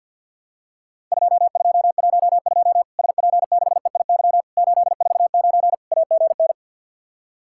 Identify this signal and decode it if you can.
Morse code, heard as a keyed tone, about 50 wpm, 700 Hz. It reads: 2212 SP6EIY 849 AGN
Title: Worked All Europe DX Contest - QTC Exchange with Repeats - 50wpm